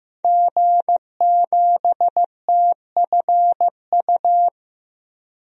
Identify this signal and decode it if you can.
Morse code, heard as a keyed tone, about 15 wpm, 700 Hz. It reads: G7TFU
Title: DXCC-100 Call Signs - 5 Characters - 15wpm